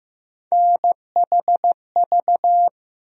Morse code, heard as a keyed tone, about 15 wpm, 700 Hz. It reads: NHV
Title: USA Call Sign Postfixes - 3 Letters 15wpm